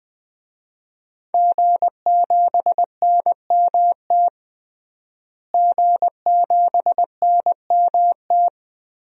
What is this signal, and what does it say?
Morse code, heard as a keyed tone, about 20 wpm, 700 Hz. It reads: G7NMT G7NMT
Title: DXCC-100 Call Signs - 5 Characters - Repeated 3x - 20wpm